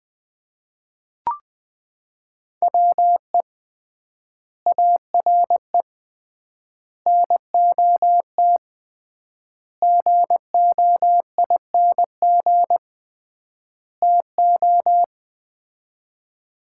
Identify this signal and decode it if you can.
Morse code, heard as a keyed tone, about 20 wpm, 700 Hz. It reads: WE ARE NOT GOING TO
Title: 3x Word-Spacing - Sets of 5 Words - 20wpm